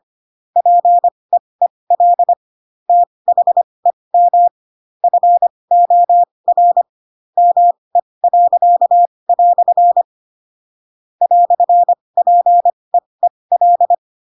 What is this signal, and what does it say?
Morse code, heard as a keyed tone, about 25 wpm, spaced as for 20 wpm, 700 Hz. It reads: PEEL THEM FOR ME." "PEEL
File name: Pinocchio / Pinocchio ch7 15 25 ik1hge